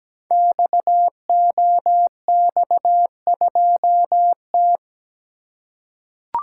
Morse code, heard as a keyed tone, about 17 wpm, 700 Hz. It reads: XOX2T
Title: Code Groups - 5 Letter:Numbers - 17wpm